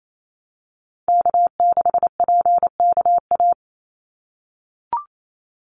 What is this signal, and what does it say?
Morse code, heard as a keyed tone, about 28 wpm, 700 Hz. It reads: K6PKA
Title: USA Call Signs 28wpm